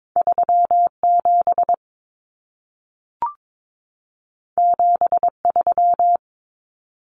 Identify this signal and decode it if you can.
Morse code, heard as a keyed tone, about 22 wpm, 700 Hz. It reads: 37 73
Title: Common Cut Numbers - 2-Digit 22wpm